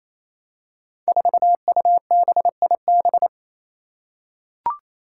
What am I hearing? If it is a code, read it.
Morse code, heard as a keyed tone, about 29 wpm, 700 Hz. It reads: 4UBIB
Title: Five Challenging Letter-Number Combinations 28wpm